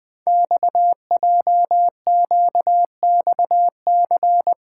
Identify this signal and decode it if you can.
Morse code, heard as a keyed tone, about 20 wpm, 700 Hz. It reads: XJQXC
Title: Code Groups - 5 Letters - 20wpm